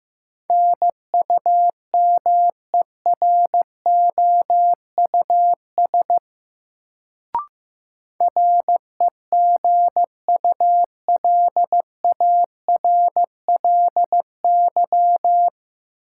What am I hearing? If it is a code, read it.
Morse code, heard as a keyed tone, about 15 wpm, 700 Hz. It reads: NUMEROUS REGULARLY
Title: Sentences from Top 1400 Words - New Word Drill - 15wpm